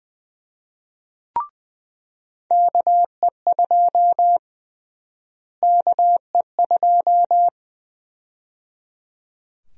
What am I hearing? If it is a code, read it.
Morse code, heard as a keyed tone, about 20 wpm, 700 Hz. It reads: KE2 KE2
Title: USA Call Sign Prefixes - Repeated 2x - 20wpm